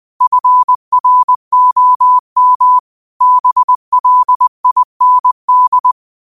Morse code, heard as a keyed tone, about 20 wpm, 1000 Hz. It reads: FROM BLIND
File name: qotdfiles / 20wpm